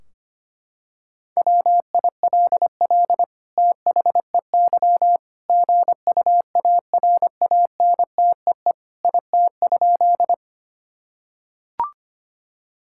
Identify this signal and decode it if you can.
Morse code, heard as a keyed tone, about 25 wpm, 700 Hz. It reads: WILL THEY GUARANTEE IT?
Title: Sentences from Top 1600 Words 25wpm